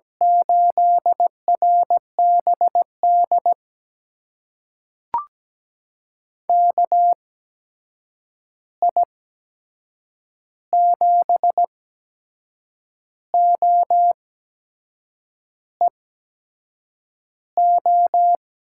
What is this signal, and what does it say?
Morse code, heard as a keyed tone, about 17 wpm, spaced as for 6 wpm, 700 Hz. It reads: 8RBD KI7OEO
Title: USA Call Signs - 8x Character Spacing - 17wpm